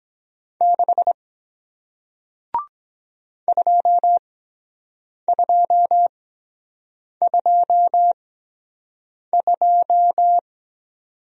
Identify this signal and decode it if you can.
Morse code, heard as a keyed tone, about 23 wpm, spaced as for 7 wpm, 700 Hz. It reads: 6 2222